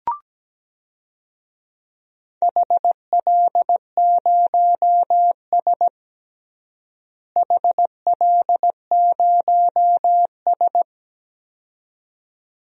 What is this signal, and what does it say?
Morse code, heard as a keyed tone, about 17 wpm, 700 Hz. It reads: HL0S HL0S
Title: DXCC-100 Call Signs - 4 Characters - Repeated 2x - 17wpm